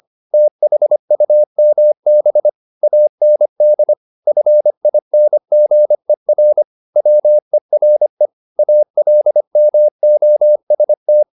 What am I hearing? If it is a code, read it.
Morse code, heard as a keyed tone, about 25 wpm, 600 Hz. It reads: THUMB AND FINGER WERE ALMOST
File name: the old man and the sea part 1 25wpm